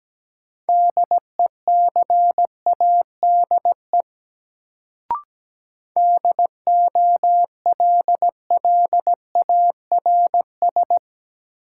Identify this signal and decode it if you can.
Morse code, heard as a keyed tone, about 17 wpm, 700 Hz. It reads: DECADE DOLLARS
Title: Sets of 5 Words - New Word Drill - 17wpm